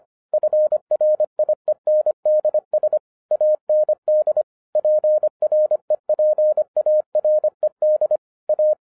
Morse code, heard as a keyed tone, about 25 wpm, 610 Hz. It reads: FRIENDS AND PREPARED A